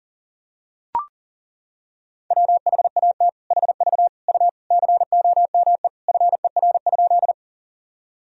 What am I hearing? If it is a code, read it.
Morse code, heard as a keyed tone, about 40 wpm, 700 Hz. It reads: WHAT HV U COME FER?
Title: Sentences from Top 100 Words and Abbreviations 40wpm